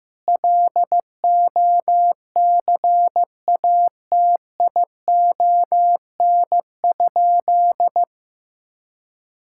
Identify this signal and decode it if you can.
Morse code, heard as a keyed tone, about 16 wpm, 700 Hz. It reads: LOCATION?
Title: Sentences from Top 1000 Words - Review Part 6 - 15wpm